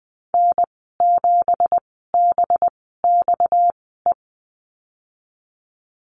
Morse code, heard as a keyed tone, about 20 wpm, spaced as for 14 wpm, 700 Hz. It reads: N7BXE
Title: 245 Lesson 39g - Letter z - Call Signs 20wpm